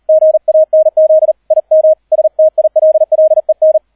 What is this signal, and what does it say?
Morse code, heard as a keyed tone, about 38 wpm, 630 Hz. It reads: GANZ IM STILLEN